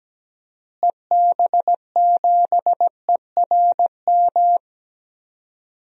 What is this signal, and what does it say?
Morse code, heard as a keyed tone, about 17 wpm, 700 Hz. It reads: EB7ERM
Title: DXCC-100 Call Signs - 6 Characters - Repeated 2x - 17wpm